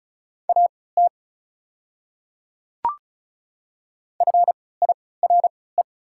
Morse code, heard as a keyed tone, about 34 wpm, spaced as for 20 wpm, 690 Hz. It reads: AT FIRE
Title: 3x Character Spacing - Top 500 Words - 35wpm